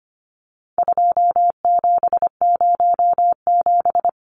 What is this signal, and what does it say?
Morse code, heard as a keyed tone, about 25 wpm, 700 Hz. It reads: 2707
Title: Numbers - 4-Digit 25wpm